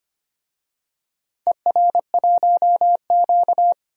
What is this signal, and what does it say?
Morse code, heard as a keyed tone, about 25 wpm, 700 Hz. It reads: ER1Q
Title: DXCC-100 Call Signs - 4 Characters - 25wpm